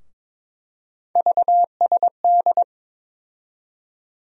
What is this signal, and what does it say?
Morse code, heard as a keyed tone, about 21 wpm, 700 Hz. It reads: VSD
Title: USA Call Sign Postfixes - 3 Letters 22wpm